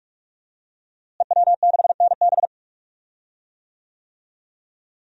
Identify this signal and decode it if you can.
Morse code, heard as a keyed tone, about 46 wpm, 700 Hz. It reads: EW6NB
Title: DXCC-100 Call Signs - 5 Characters - Repeated 3x - 45wpm